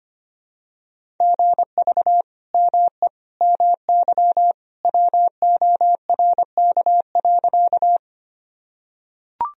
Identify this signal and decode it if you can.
Morse code, heard as a keyed tone, about 25 wpm, 700 Hz. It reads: GV ME MY WORK.
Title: Sentences from Top 200 Words and Abbreviations 25wpm